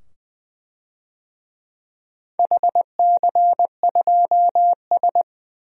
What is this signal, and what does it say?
Morse code, heard as a keyed tone, about 20 wpm, 700 Hz. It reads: HC2S
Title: DXCC-100 Call Signs - 4 Characters - 20wpm